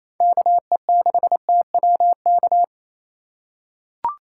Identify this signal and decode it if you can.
Morse code, heard as a keyed tone, about 28 wpm, 700 Hz. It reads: KE6TWK
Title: USA Call Signs - 6 Characters - 3x Character Spacing - 28wpm